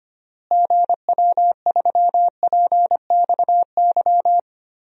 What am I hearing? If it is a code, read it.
Morse code, heard as a keyed tone, about 25 wpm, 700 Hz. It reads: GW3PXY
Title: DXCC-100 Call Signs - 6 Characters - Repeated 3x - 25wpm